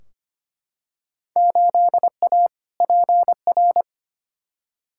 Morse code, heard as a keyed tone, about 25 wpm, 700 Hz. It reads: 8A PR
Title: Field Day Exchanges Encore - 25wpm